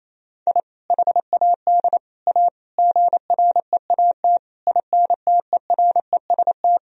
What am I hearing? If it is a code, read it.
Morse code, heard as a keyed tone, about 28 wpm, 700 Hz. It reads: I HAD A GREAT INTEREST
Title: Sentences from Top 1000 Words - Review Part 2 - 28wpm